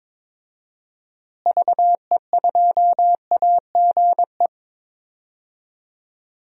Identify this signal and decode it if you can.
Morse code, heard as a keyed tone, about 22 wpm, 700 Hz. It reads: VE2AGE
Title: DXCC-100 Call Signs - 6 Characters - Repeated 3x - 22wpm